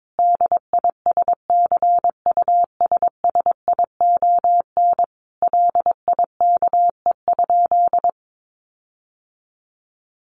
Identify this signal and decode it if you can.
Morse code, heard as a keyed tone, about 22 wpm, 700 Hz. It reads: DISCUSSION LIKE?
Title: Sentences from Top 1500 Words 22wpm